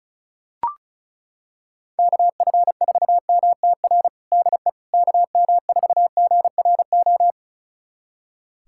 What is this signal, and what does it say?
Morse code, heard as a keyed tone, about 36 wpm, 700 Hz. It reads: KF4MTR DE KM4GRO
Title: USA Callsign DE Callsign 35wpm